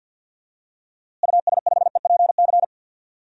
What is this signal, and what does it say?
Morse code, heard as a keyed tone, about 53 wpm, 700 Hz. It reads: US5EPC
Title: DXCC-100 Call Signs - 6 Characters - Repeated 3x - 50wpm